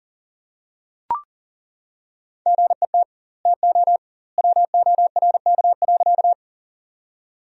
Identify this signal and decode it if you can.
Morse code, heard as a keyed tone, about 40 wpm, 700 Hz. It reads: GET TO WORK.
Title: 2x Word-Spacing - Sentences from Top 200 Words - 40wpm